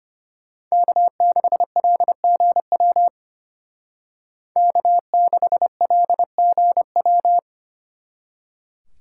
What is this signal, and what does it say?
Morse code, heard as a keyed tone, about 33 wpm, 700 Hz. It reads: K6LGW K6LGW